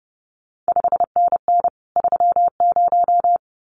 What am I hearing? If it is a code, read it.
Morse code, heard as a keyed tone, about 30 wpm, 700 Hz. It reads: 5NN 30
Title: CQ WW DX Contest Exchange Encore - 30wpm